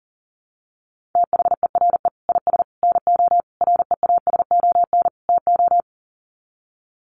Morse code, heard as a keyed tone, about 40 wpm, 700 Hz. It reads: THERE IS NO REASON TO
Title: Sets of 5 Words Encore 40wpm